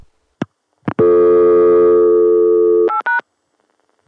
Dial tone (North America), 350 + 440 Hz.